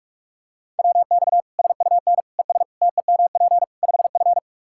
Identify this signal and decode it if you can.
Morse code, heard as a keyed tone, about 43 wpm, 700 Hz. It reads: WX SUN ES TEMP 5F
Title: Practice QSO - Exchange 2 - Info - 45wpm